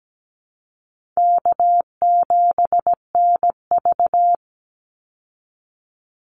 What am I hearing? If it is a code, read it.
Morse code, heard as a keyed tone, about 17 wpm, 700 Hz. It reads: K7NV
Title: Easy Call Signs Repeated 3x - 17wpm